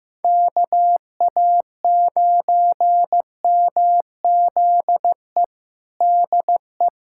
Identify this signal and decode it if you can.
Morse code, heard as a keyed tone, about 15 wpm, 700 Hz. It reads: KA9MZE DE
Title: USA Callsign DE Callsign 15wpm